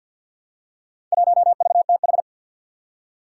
Morse code, heard as a keyed tone, about 47 wpm, 700 Hz. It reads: 1VTH
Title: Four Challenging Letter-Number Combinations 50wpm